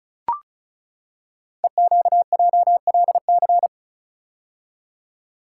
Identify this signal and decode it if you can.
Morse code, heard as a keyed tone, about 36 wpm, 700 Hz. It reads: EQJLC
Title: Code Groups - 5 Letter:Numbers - 35wpm